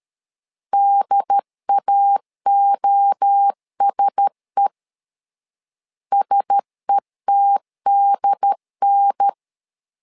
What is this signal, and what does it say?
Morse code, heard as a keyed tone, about 13 wpm, 800 Hz. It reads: DAOSE SETDN
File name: LEZIONI / 4 Q1S 12wpm 1802